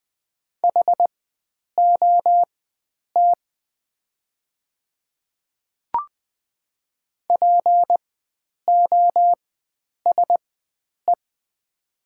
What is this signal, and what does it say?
Morse code, heard as a keyed tone, about 20 wpm, spaced as for 9 wpm, 700 Hz. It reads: HOT POSE
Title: 4x Character Spacing - Top 500 Words - 20wpm